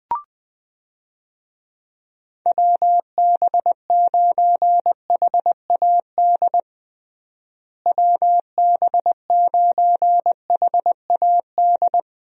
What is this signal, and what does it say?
Morse code, heard as a keyed tone, about 20 wpm, 700 Hz. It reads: WB9HAD WB9HAD
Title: USA Call Signs - 6 Characters - Repeated 3x - 20wpm